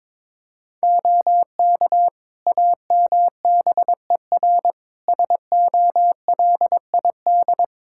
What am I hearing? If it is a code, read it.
Morse code, heard as a keyed tone, about 22 wpm, 700 Hz. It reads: OK AMBER SOLID
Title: Practice QSO - Exchange 3 - More Info - 22wpm